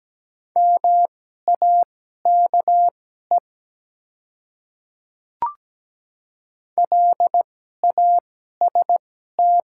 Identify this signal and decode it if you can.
Morse code, heard as a keyed tone, about 17 wpm, spaced as for 12 wpm, 700 Hz. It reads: MAKE LAST
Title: 2x Character Spacing - Top 300 Words - 17wpm